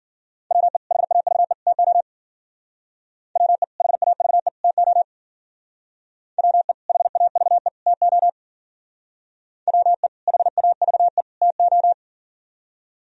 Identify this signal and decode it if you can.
Morse code, heard as a keyed tone, about 53 wpm, 700 Hz. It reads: WE HAVE TO WE HAVE TO WE HAVE TO WE HAVE TO